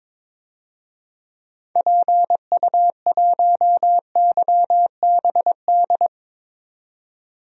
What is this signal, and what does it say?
Morse code, heard as a keyed tone, about 22 wpm, 700 Hz. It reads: PU1YBD
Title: DXCC-100 Call Signs - 6 Characters - 22wpm